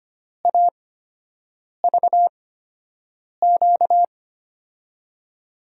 Morse code, heard as a keyed tone, about 25 wpm, spaced as for 7 wpm, 700 Hz. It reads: AVQ
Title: USA Call Signs - 6 Characters - 8x Character Spacing - 25wpm